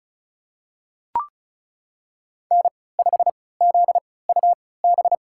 Morse code, heard as a keyed tone, about 36 wpm, spaced as for 20 wpm, 700 Hz. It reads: N5ZUB
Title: USA Call Signs - 5 Characters - 3x Character Spacing - 35wpm